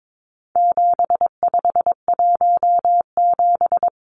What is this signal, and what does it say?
Morse code, heard as a keyed tone, about 22 wpm, 700 Hz. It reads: 7517
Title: Numbers - 4-Digit 22wpm